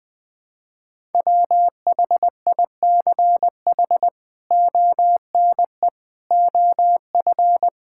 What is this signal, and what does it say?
Morse code, heard as a keyed tone, about 20 wpm, 700 Hz. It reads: WHICH ONE OF
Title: Sentences from Top 1000 Words - Review Part 5 - 20wpm